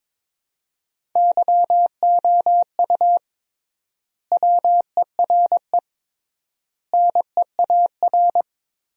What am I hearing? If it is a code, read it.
Morse code, heard as a keyed tone, about 22 wpm, 700 Hz. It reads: YOU WERE NEAR
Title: 3x Word-Spacing - Sentences from Top 200 Words - 22wpm